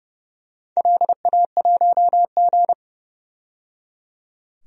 Morse code, heard as a keyed tone, about 30 wpm, 700 Hz. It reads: LA1G
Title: DXCC-100 Call Signs - 4 Characters - 30wpm